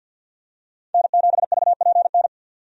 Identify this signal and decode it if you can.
Morse code, heard as a keyed tone, about 50 wpm, 700 Hz. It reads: N7VPN
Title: USA Call Signs - 5 Characters - 50wpm